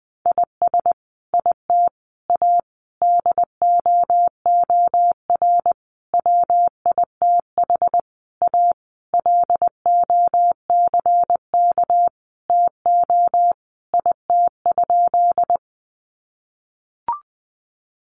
Morse code, heard as a keyed tone, about 19 wpm, 700 Hz. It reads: IS IT A DOOR WITH A LOCK TO IT?
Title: Sentences from Top 1300 Words 20wpm